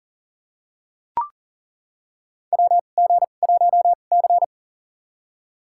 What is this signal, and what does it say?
Morse code, heard as a keyed tone, about 40 wpm, spaced as for 29 wpm, 700 Hz. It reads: WG1C
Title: USA Call Signs - 2x Character Spacing - 40wpm